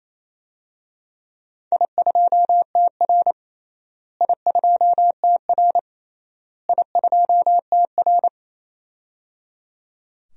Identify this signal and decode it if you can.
Morse code, heard as a keyed tone, about 28 wpm, 700 Hz. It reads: I2TR I2TR I2TR
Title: DXCC-100 Call Signs - 4 Characters - Repeated 3x - 28wpm